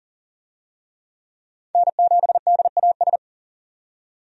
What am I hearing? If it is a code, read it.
Morse code, heard as a keyed tone, about 40 wpm, 700 Hz. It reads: N7DAS
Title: USA Call Signs - 5 Characters - 40wpm